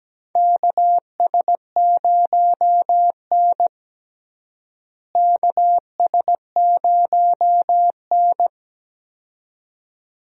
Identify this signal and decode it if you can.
Morse code, heard as a keyed tone, about 17 wpm, 700 Hz. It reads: KS0N KS0N
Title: USA Call Signs - 4 Characters - Repeated 2x - 17wpm